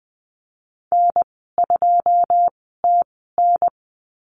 Morse code, heard as a keyed tone, about 20 wpm, spaced as for 15 wpm, 700 Hz. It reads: N2TN
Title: 122 Lesson 20g - Number 3 - Call Signs 20wpm